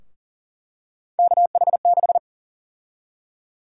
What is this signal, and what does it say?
Morse code, heard as a keyed tone, about 40 wpm, 700 Hz. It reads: KH6
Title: USA Call Sign Prefixes - 40wpm